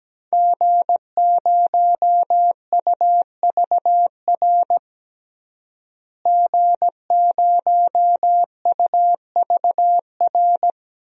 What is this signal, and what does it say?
Morse code, heard as a keyed tone, about 17 wpm, 700 Hz. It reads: G0UVR G0UVR
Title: DXCC-100 Call Signs - 5 Characters - Repeated 3x - 17wpm